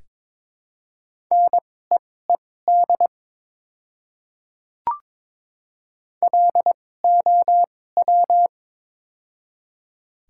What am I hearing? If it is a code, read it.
Morse code, heard as a keyed tone, about 22 wpm, spaced as for 16 wpm, 700 Hz. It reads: NEED LOW
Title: 2x Character Spacing - Top 300 Words - 22wpm